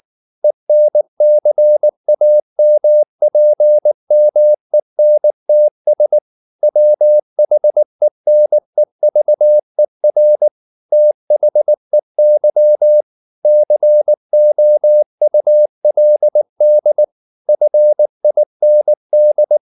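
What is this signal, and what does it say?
Morse code, heard as a keyed tone, about 19 wpm, 600 Hz. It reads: ENCAMPMENTS WHENEVER THEY COULD FIND
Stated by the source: Morse code at 19 wpm from CW med Gnister, reading Ken Follett's novel Never in English.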